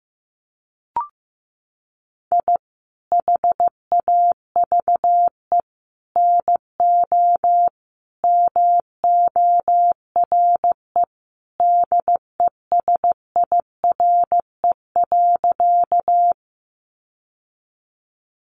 Morse code, heard as a keyed tone, about 15 wpm, 700 Hz. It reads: I HAVE NO MORE DESIRE.
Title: Sentences from Top 1100 Words 15wpm